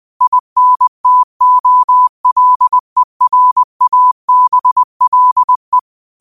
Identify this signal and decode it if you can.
Morse code, heard as a keyed tone, about 20 wpm, 1000 Hz. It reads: INTOLERABLE